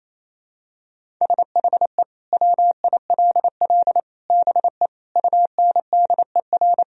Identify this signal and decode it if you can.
Morse code, heard as a keyed tone, about 27 wpm, 700 Hz. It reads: SHE WILL BE UNDER
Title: Sentences from Top 1000 Words - Review Part 1 - 28wpm